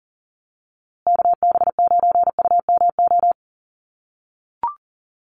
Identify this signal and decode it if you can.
Morse code, heard as a keyed tone, about 40 wpm, 700 Hz. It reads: KB9UMO
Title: USA Call Signs Repeated 2x 40wpm